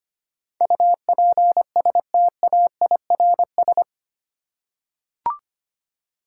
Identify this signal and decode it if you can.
Morse code, heard as a keyed tone, about 25 wpm, 700 Hz. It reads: UPSTAIRS
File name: 8 Letter Words - First 3 Letters Given - 25wpm